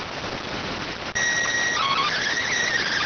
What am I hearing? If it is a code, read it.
SSTV (slow-scan television), PD 290.